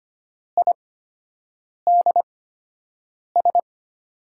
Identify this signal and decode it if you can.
Morse code, heard as a keyed tone, about 25 wpm, spaced as for 7 wpm, 700 Hz. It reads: IDS
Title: USA Call Signs - 5 Characters - 8x Character Spacing - 25wpm